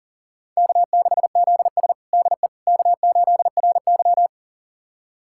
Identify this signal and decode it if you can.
Morse code, heard as a keyed tone, about 40 wpm, 700 Hz. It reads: K6ZS DE K8RY
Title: USA Callsign DE Callsign - 4 Characters - 40wpm